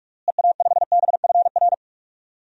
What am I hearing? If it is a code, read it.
Morse code, heard as a keyed tone, about 46 wpm, 700 Hz. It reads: EA5BFR
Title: DXCC-100 Call Signs - 6 Characters - Repeated 3x - 45wpm